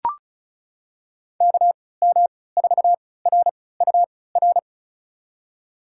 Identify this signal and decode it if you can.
Morse code, heard as a keyed tone, about 36 wpm, spaced as for 20 wpm, 700 Hz. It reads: KM4RUR
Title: USA Call Signs - 6 Characters - 3x Character Spacing - 35wpm